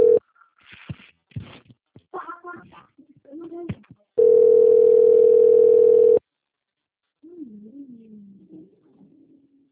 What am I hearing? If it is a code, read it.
Ringback tone (North America), 440 + 480 Hz.